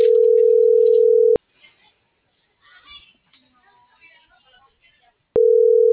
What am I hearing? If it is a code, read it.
Ringback tone (North America), 440 + 480 Hz.
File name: CENTRAL FRALSE / out brendaaviles.j